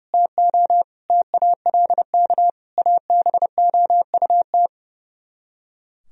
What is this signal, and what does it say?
Morse code, heard as a keyed tone, about 38 wpm, 690 Hz. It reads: TO TALK ABOUT